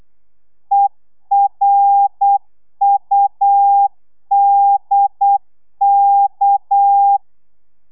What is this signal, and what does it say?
Morse code, heard as a keyed tone, about 8 wpm, 800 Hz. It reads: ERUDK
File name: telegrafia / i2tfj / Lez6 35